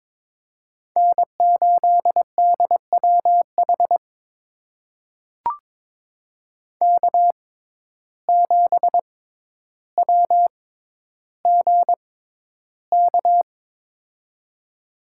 Morse code, heard as a keyed tone, about 22 wpm, spaced as for 11 wpm, 700 Hz. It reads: N8DWH K7WGK
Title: USA Call Signs - 5 Characters - 6x Character Spacing - 22wpm